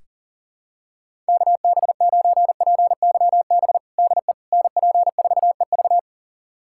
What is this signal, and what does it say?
Morse code, heard as a keyed tone, about 40 wpm, 700 Hz. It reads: KB9PYB DE NP4EV
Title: USA Callsign DE Callsign 40wpm